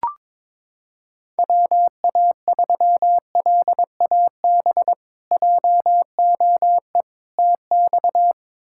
Morse code, heard as a keyed tone, about 21 wpm, 700 Hz. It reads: WA3LAB JOE TX
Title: K1USN SST QSO Practice - 22wpm